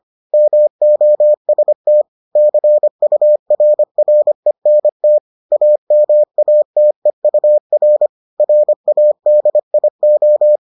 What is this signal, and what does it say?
Morse code, heard as a keyed tone, about 25 wpm, 600 Hz. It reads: MOST CURRENT AMATEUR RADIO